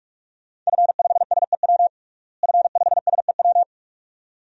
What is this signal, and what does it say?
Morse code, heard as a keyed tone, about 46 wpm, 700 Hz. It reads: F5SEW F5SEW
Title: DXCC-100 Call Signs - 5 Characters - Repeated 2x - 45wpm